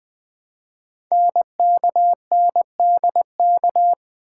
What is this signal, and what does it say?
Morse code, heard as a keyed tone, about 20 wpm, 700 Hz. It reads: NKNDK
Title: Code Groups - 5 Letters - 20wpm